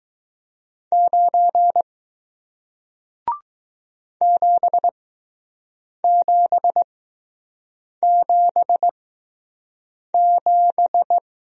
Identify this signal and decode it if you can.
Morse code, heard as a keyed tone, about 20 wpm, spaced as for 6 wpm, 700 Hz. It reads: 9 7777